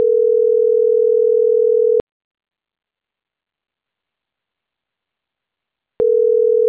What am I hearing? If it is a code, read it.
Ringback tone (North America), 440 + 480 Hz.